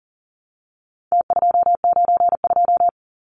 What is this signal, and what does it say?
Morse code, heard as a keyed tone, about 40 wpm, 700 Hz. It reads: T292
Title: Numbers - Contest Style - 4-Digit 40wpm